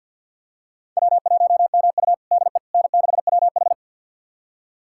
Morse code, heard as a keyed tone, about 46 wpm, 700 Hz. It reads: W1MU DE N6WH
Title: USA Callsign DE Callsign - 4 Characters - 50wpm